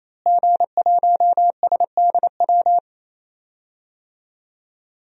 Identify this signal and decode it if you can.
Morse code, heard as a keyed tone, about 28 wpm, 700 Hz. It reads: G1SDW
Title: DXCC-100 Call Signs - 5 Characters - Repeated 3x - 28wpm